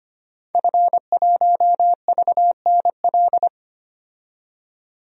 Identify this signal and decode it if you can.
Morse code, heard as a keyed tone, about 25 wpm, 700 Hz. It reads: F1VNL
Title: DXCC-100 Call Signs - 5 Characters - 25wpm